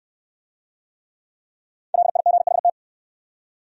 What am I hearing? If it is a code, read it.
Morse code, heard as a keyed tone, about 67 wpm, 700 Hz. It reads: LIGHT